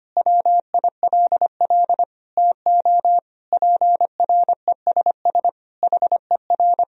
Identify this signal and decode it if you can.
Morse code, heard as a keyed tone, about 25 wpm, 700 Hz. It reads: WILL TO PRESS HER
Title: Sentences from Top 1000 Words - Review Part 1 - 25wpm